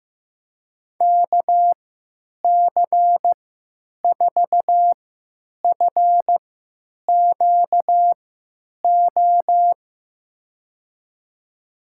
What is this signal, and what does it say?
Morse code, heard as a keyed tone, about 15 wpm, spaced as for 9 wpm, 700 Hz. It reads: KC4FQO
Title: USA Call Signs - 6 Characters - 3x Character Spacing - 15wpm